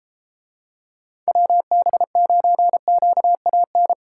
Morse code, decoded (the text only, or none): WB9QAN